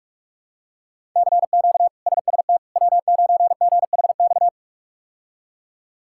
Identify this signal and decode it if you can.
Morse code, heard as a keyed tone, about 44 wpm, 700 Hz. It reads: CQ SST W9GHX
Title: K1USN SST QSO Practice - 45wpm